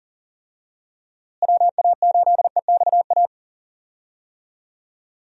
Morse code, heard as a keyed tone, about 40 wpm, 700 Hz. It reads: WA8EXA
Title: USA Call Signs - 6 Characters - Repeated 2x - 40wpm